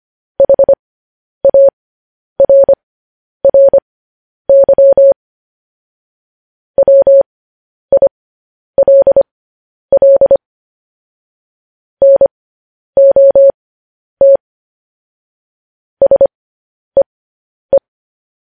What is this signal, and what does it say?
Morse code, heard as a keyed tone, about 25 wpm, spaced as for 10 wpm, 550 Hz. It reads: HARRY WILL NOT SEE
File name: podcast / News headlines-1x-25x10